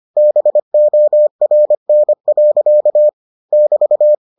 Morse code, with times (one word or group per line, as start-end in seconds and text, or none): 0.2-3.1 BORN.
3.5-4.1 <BT>